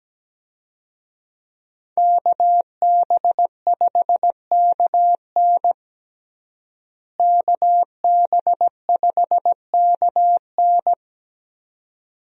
Morse code, heard as a keyed tone, about 17 wpm, 700 Hz. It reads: KB5KN KB5KN
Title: USA Call Signs - 5 Characters - Repeated 2x - 17wpm